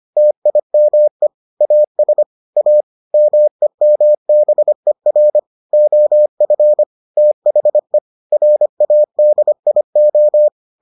Morse code, heard as a keyed tone, about 25 wpm, 600 Hz. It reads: TIME AS A MEMBER OF THE RADIO